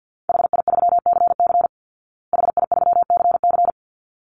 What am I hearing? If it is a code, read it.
Morse code, heard as a keyed tone, about 50 wpm, 700 Hz. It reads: HI3CC HI3CC
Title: Call Signs Repeated 3x - 50wpm